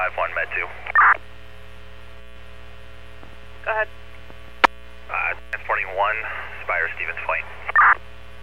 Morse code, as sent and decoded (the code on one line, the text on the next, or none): none